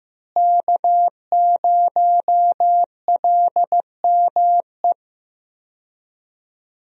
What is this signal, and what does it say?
Morse code, heard as a keyed tone, about 15 wpm, 700 Hz. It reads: K0LME
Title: USA Call Signs - 5 Characters - Repeated 2x - 15wpm